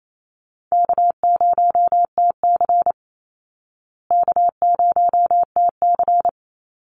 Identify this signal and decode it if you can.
Morse code, heard as a keyed tone, about 28 wpm, 700 Hz. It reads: K0TC K0TC
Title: Call Signs Repeated 3x - 28wpm